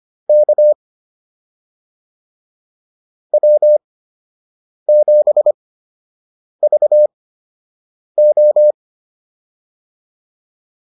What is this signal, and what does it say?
Morse code, heard as a keyed tone, about 25 wpm, spaced as for 7 wpm, 600 Hz. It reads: K W7VO